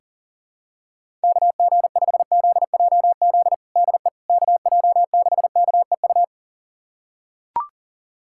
Morse code, heard as a keyed tone, about 40 wpm, 700 Hz. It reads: KG5ZJZ DE KJ6KEU
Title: USA Callsign DE Callsign - 6 Characters - 40wpm